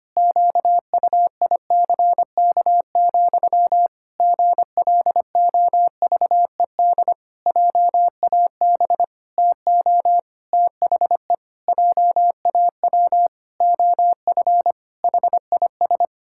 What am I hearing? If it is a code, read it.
Morse code, heard as a keyed tone, about 25 wpm, 700 Hz. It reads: QUICK, GLOVED JAB TO THE JAW OF HIS
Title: Pangram Challenge v2 - 25wpm